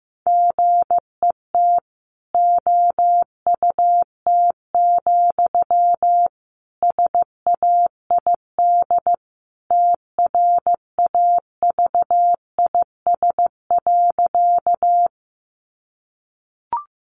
Morse code, heard as a keyed tone, about 15 wpm, 700 Hz. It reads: GET OUT, SAID TRAVIS.
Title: A Sound Of Thunder 15wpm